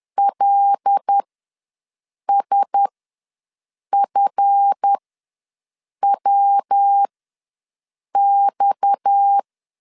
Morse code, heard as a keyed tone, about 10 wpm, spaced as for 6 wpm, 800 Hz. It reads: LSFWX
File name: LEZIONI / 7 GF9Z / 7 GF ws 10wpm 3918